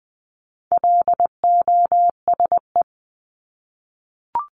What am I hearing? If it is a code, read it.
Morse code, heard as a keyed tone, about 20 wpm, 700 Hz. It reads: LOSE
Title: Top 1001-1250 Words 20wpm